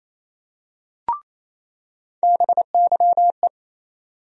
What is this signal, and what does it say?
Morse code, heard as a keyed tone, about 28 wpm, 700 Hz. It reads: BYE
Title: USA Call Sign Postfixes - 3 Letters 28wpm